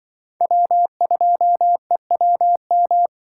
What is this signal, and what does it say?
Morse code, heard as a keyed tone, about 24 wpm, 700 Hz. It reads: W2EWM
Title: Speed Racing - USA 1x3 Call Signs 20wpm